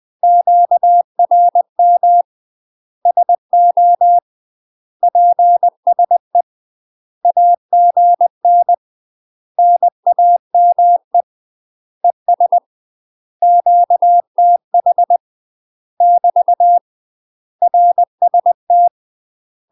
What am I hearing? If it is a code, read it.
Morse code, heard as a keyed tone, about 20 wpm, 700 Hz. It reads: QRM SO PSE AGN NAME ES QTH <BT> RST
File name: UK MASTER QSOs TRAINING / 20WPM / UK MASTER QSOs 0066